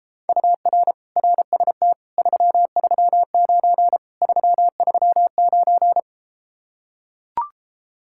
Morse code, heard as a keyed tone, about 33 wpm, 700 Hz. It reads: UR RST 339 339